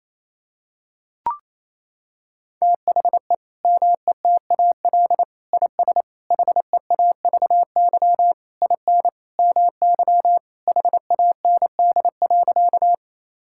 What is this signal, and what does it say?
Morse code, heard as a keyed tone, about 28 wpm, 700 Hz. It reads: THE METAL IS HEAVY IN MY HAND.
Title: Sentences from Top 1000 Words - Review Part 6 - 28wpm